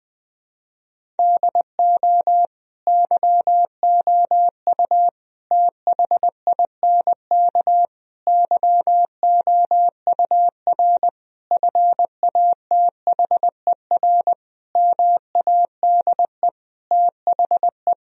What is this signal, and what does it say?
Morse code, heard as a keyed tone, about 21 wpm, 700 Hz. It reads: DO YOU THINK YOUR FATHER MADE THE
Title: Sentences from Top 1000 Words - Review Part 1 - 20wpm